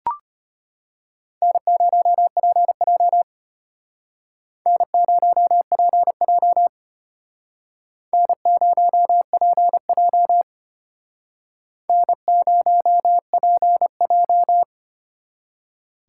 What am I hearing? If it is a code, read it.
Morse code, heard as a keyed tone, about 32 wpm, 700 Hz. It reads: N0PJ N0PJ N0PJ N0PJ